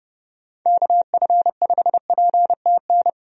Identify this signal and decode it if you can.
Morse code, heard as a keyed tone, about 30 wpm, 700 Hz. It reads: KF5PTN
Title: USA Call Signs - 6 Characters - 30wpm